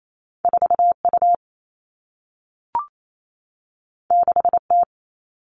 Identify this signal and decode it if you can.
Morse code, heard as a keyed tone, about 28 wpm, 700 Hz. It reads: 4U 6T
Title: All Cut Numbers - 2-Digit 28wpm